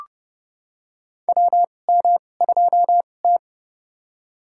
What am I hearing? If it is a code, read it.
Morse code, heard as a keyed tone, about 31 wpm, spaced as for 22 wpm, 700 Hz. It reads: WM2T
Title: USA Call Signs - 4 Characters - 2x Character Spacing - 30wpm